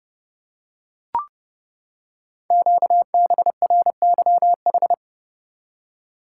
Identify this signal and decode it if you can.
Morse code, heard as a keyed tone, about 30 wpm, 690 Hz. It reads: QBRYH
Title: Code Groups - 5 Letters - 30wpm